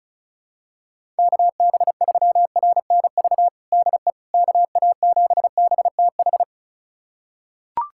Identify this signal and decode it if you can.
Morse code, heard as a keyed tone, about 34 wpm, 700 Hz. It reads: KB3RNV DE KA7BTH
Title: USA Callsign DE Callsign 35wpm